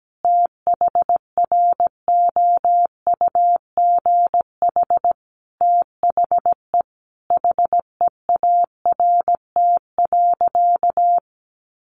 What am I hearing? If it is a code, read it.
Morse code, heard as a keyed tone, about 17 wpm, 700 Hz. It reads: THROUGH THE HEART.
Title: Sentences from Top 1200 Words 17wpm